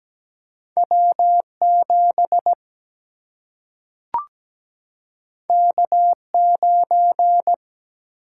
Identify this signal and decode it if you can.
Morse code, heard as a keyed tone, about 17 wpm, 700 Hz. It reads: W7 K9
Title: Single-Letter Single-Number 17wpm